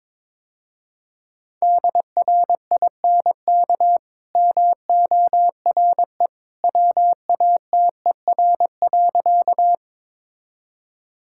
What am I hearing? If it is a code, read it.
Morse code, heard as a keyed tone, about 22 wpm, 690 Hz. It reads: DRINK MORE WATER.
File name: practice / 22wpm / Easy Sentences with Hints - 22wpm